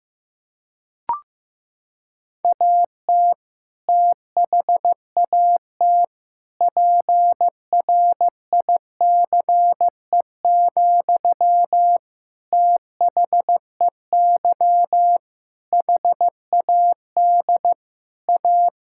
Morse code, heard as a keyed tone, about 15 wpm, 700 Hz. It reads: AT THAT PRICE, THEY HAD A
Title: Sentences from Top 1000 Words - Review Part 5 - 15wpm